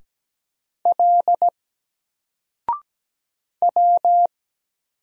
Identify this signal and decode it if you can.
Morse code, heard as a keyed tone, about 17 wpm, 700 Hz. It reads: L W
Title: Single Character (Letter:Number:Punctuation:Prosign) 17wpm